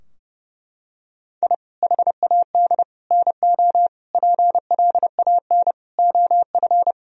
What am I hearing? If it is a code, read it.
Morse code, heard as a keyed tone, about 30 wpm, 700 Hz. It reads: I HAD NO PLAN OF
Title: Sentences from Top 1000 Words - Review Part 2 - 30wpm